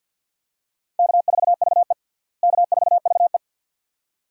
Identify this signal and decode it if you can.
Morse code, heard as a keyed tone, about 52 wpm, 700 Hz. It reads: K4VE K4VE
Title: USA Call Signs - 4 Characters - Repeated 2x - 50wpm